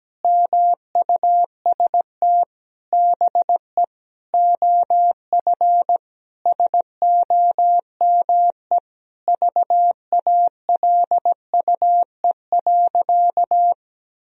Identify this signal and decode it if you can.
Morse code, heard as a keyed tone, about 17 wpm, 700 Hz. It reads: MUST BE OF SOME VALUE.
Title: Sentences from Top 1000 Words - Review Part 3 - 17wpm